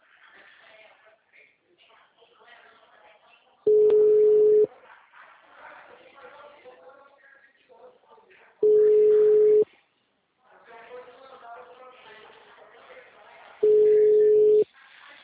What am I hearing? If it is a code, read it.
Ringback tone (Europe, 425 Hz), with speech.
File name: out- -2015- - - .553